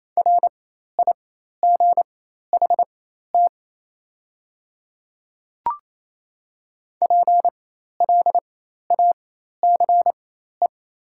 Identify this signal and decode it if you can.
Morse code, heard as a keyed tone, about 28 wpm, spaced as for 13 wpm, 700 Hz. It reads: RIGHT PLACE
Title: 4x Character Spacing - Top 200 Words - 28wpm